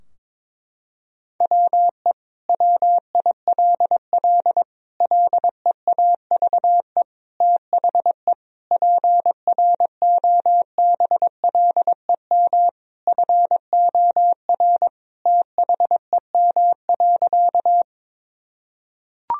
Morse code, heard as a keyed tone, about 22 wpm, 700 Hz. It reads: WE WILL LEAVE THE PROBLEM FOR THEM.
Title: Sentences from Top 1000 Words - Review Part 2 - 22wpm